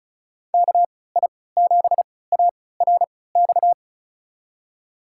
Morse code, heard as a keyed tone, about 34 wpm, spaced as for 20 wpm, 700 Hz. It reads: KI7ARX
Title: USA Call Signs - 6 Characters - 3x Character Spacing - 35wpm